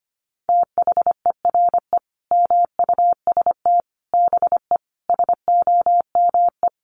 Morse code, heard as a keyed tone, about 25 wpm, 700 Hz. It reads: THERE MUST BE SOME
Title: Sentences from Top 700 Words 25wpm